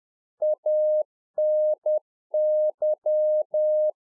Morse code, heard as a keyed tone, about 10 wpm, 610 Hz. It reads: ANY
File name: 2011-04-mp3 / UNFOGETTABLE QSO 10WPM